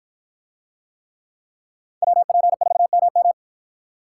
Morse code, heard as a keyed tone, about 53 wpm, 700 Hz. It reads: WP4MM